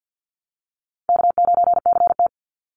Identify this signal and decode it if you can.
Morse code, heard as a keyed tone, about 50 wpm, 700 Hz. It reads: K9CT
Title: Easy Call Signs Repeated 2x - 50wpm